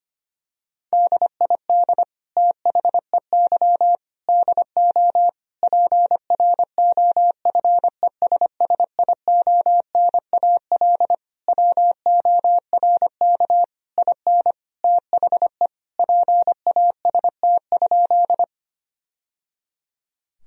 Morse code, heard as a keyed tone, about 25 wpm, 700 Hz. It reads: DID THEY DO PROFESSIONAL WORK IN THE PAST?
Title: Sentences from Top 1000 Words - Review Part 4 - 25wpm